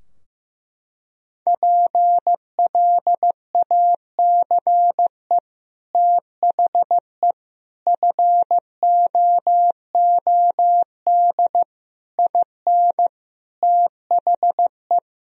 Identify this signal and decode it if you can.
Morse code, heard as a keyed tone, about 15 wpm, 700 Hz. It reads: PLACE THE FOOD IN THE
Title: Sentences from Top 1000 Words - Review Part 4 - 15wpm